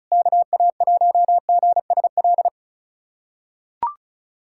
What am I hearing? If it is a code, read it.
Morse code, heard as a keyed tone, about 36 wpm, 700 Hz. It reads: KA1GSL
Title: USA Callsign DE Callsign - 6 Characters - 35wpm